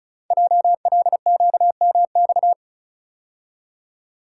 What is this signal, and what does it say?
Morse code, heard as a keyed tone, about 36 wpm, 690 Hz. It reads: JLQMX
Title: Code Groups - 5 Letters - 35wpm